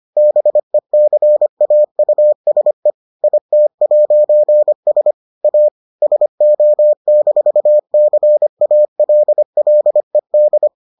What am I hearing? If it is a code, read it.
Morse code, heard as a keyed tone, about 25 wpm, 600 Hz. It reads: BECAUSE IT'S A SO-CALLED